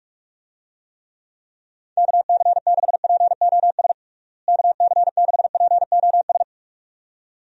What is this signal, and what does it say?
Morse code, heard as a keyed tone, about 43 wpm, 700 Hz. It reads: KC6POS KC6POS
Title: USA Call Signs - 6 Characters - Repeated 2x - 45wpm